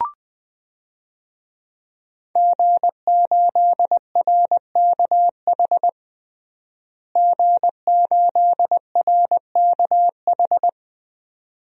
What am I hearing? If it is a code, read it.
Morse code, heard as a keyed tone, about 20 wpm, 700 Hz. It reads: G8RKH G8RKH
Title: DXCC-100 Call Signs - 5 Characters - Repeated 2x - 20wpm